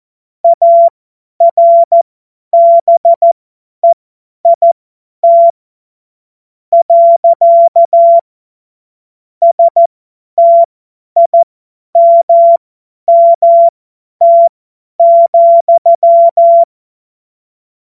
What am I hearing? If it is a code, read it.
Morse code, heard as a keyed tone, about 14 wpm, spaced as for 9 wpm, 680 Hz. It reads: ARBEIT. STIMMT,
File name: Witz-08-Juli-2025-50-F70